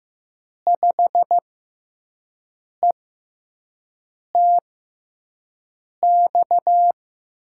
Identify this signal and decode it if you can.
Morse code, heard as a keyed tone, about 15 wpm, spaced as for 5 wpm, 700 Hz. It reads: 5ETX
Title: USA Call Signs - 6x Character Spacing - 15wpm